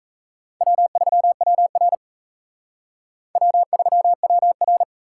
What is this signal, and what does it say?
Morse code, heard as a keyed tone, about 35 wpm, 700 Hz. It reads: W3WR W3WR
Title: Speed Racing - USA 1x2 Call Signs 28wpm